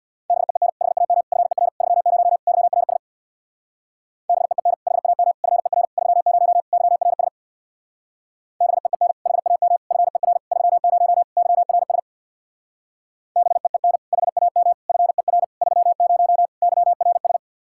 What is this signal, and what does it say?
Morse code, heard as a keyed tone, about 63 wpm, 700 Hz. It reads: BEEN HAM FER 30 YRS BEEN HAM FER 30 YRS BEEN HAM FER 30 YRS BEEN HAM FER 30 YRS